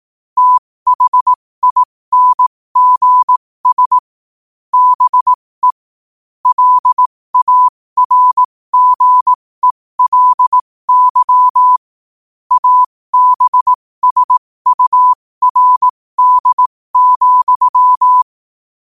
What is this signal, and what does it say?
Morse code, heard as a keyed tone, about 18 wpm, spaced as for 15 wpm, 1000 Hz. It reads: THINGS BE LARGELY ABSURD,
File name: qotdfiles / 15wpm